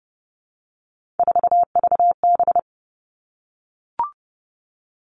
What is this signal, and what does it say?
Morse code, heard as a keyed tone, about 30 wpm, 700 Hz. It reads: 4VB
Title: Three Challenging Letter-Number Combinations 30wpm